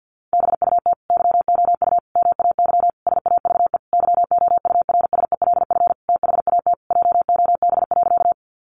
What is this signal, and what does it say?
Morse code, heard as a keyed tone, about 50 wpm, 700 Hz. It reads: BUT YOU MAY SAVE YOURSELF THAT JOB.
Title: Sentences from Top 1100 Words 50wpm